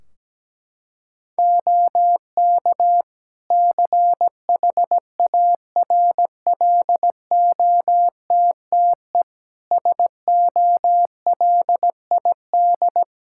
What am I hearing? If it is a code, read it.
Morse code, heard as a keyed tone, about 17 wpm, 700 Hz. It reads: OK CHARLOTTE SOLID
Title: Practice QSO - Exchange 3 - More Info - 17wpm